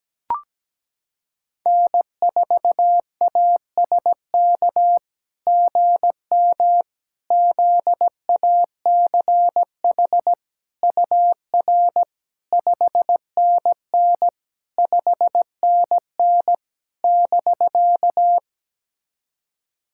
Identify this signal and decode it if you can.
Morse code, heard as a keyed tone, about 17 wpm, 700 Hz. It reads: N4ASK GM ZACH UR 5NN 5NN <BK>
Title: SOTA QSO Practice 17wpm